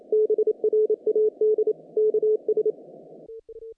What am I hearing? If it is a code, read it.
Morse code, heard as a keyed tone, about 27 wpm, 430 Hz. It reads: BRAD KS
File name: contests / naqp-cw-k5tr-op-k5pi- -1800